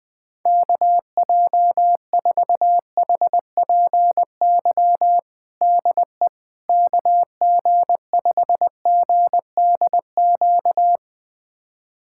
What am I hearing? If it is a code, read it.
Morse code, heard as a keyed tone, about 20 wpm, 700 Hz. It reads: KJ4HPY DE KG5GDQ
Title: USA Callsign DE Callsign 20wpm